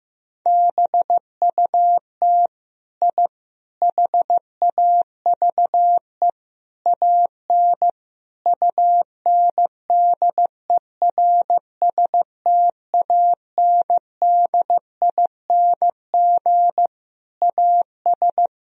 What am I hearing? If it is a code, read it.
Morse code, heard as a keyed tone, about 16 wpm, 700 Hz. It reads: BUT I HAVE AN UNDERSTANDING AS
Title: Sentences from Top 1000 Words - Review Part 3 - 15wpm